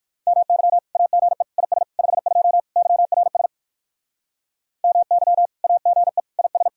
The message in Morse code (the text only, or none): MY AGE IS 52 YRS MY AGE IS